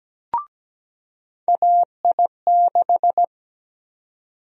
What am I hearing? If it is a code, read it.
Morse code, heard as a keyed tone, about 17 wpm, 700 Hz. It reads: AI6
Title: USA Call Sign Prefixes - 17wpm